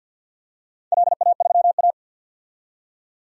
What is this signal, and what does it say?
Morse code, heard as a keyed tone, about 53 wpm, 700 Hz. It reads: LA3A
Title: DXCC-100 Call Signs - 4 Characters - 50wpm